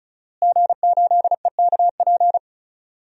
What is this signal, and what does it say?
Morse code, heard as a keyed tone, about 36 wpm, 700 Hz. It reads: G8EKP
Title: DXCC-100 Call Signs - 5 Characters - Repeated 3x - 35wpm